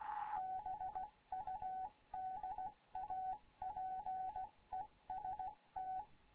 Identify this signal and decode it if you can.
Morse code, heard as a keyed tone, about 16 wpm, 750 Hz, with radio noise. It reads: BUDAPEST
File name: Morze-HA2MN / hangp / 2 11 GYAKORLAT